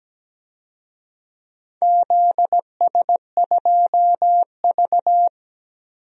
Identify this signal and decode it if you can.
Morse code, heard as a keyed tone, about 17 wpm, 700 Hz. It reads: ZS2V
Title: DXCC-100 Call Signs - 4 Characters - 17wpm